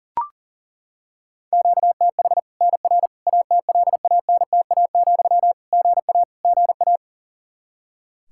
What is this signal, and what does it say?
Morse code, heard as a keyed tone, about 40 wpm, 700 Hz. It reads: QTH NR ATLANTA, GA GA
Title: Practice QSO - Exchange 1 - Beginning - 40wpm